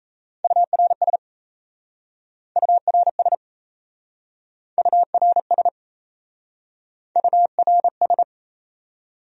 Morse code, heard as a keyed tone, about 38 wpm, 700 Hz. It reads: URS URS URS URS